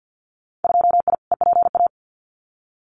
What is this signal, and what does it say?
Morse code, heard as a keyed tone, about 50 wpm, 700 Hz. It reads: 2I EPA
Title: Winter Field Day Exchanges Encore 50wpm